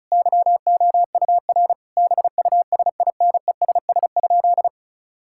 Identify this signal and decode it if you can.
Morse code, heard as a keyed tone, about 36 wpm, 690 Hz. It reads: YOUR BUSINESS?
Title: Sentences from Top 1000 Words - Review Part 6 - 35wpm